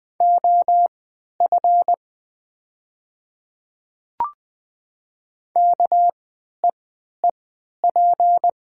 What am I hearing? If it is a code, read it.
Morse code, heard as a keyed tone, about 20 wpm, spaced as for 11 wpm, 700 Hz. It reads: OF KEEP
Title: 3x Character Spacing - Top 300 Words - 20wpm